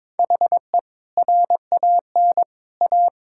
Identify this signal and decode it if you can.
Morse code, heard as a keyed tone, about 22 wpm, 700 Hz. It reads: HE RAN A
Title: Sentences from Top 1000 Words - Review Part 6 - 22wpm